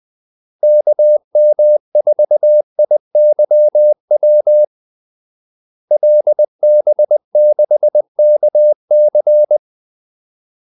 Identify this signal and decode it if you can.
Morse code, heard as a keyed tone, about 20 wpm, 600 Hz. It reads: KM4IYW LB6KC
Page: Kaldesignaler 20 wpm